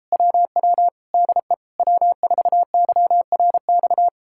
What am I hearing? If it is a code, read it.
Morse code, heard as a keyed tone, about 38 wpm, 700 Hz. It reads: WW DE W4YRX